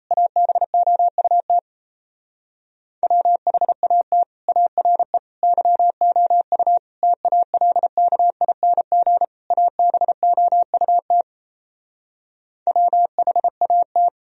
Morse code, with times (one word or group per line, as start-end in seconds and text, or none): 0.1-1.6 ABOUT
3.0-4.2 WHAT
4.5-5.2 ARE
5.4-6.8 YOU
7.0-9.2 TALKING
9.5-11.2 ABOUT
12.7-14.1 WHAT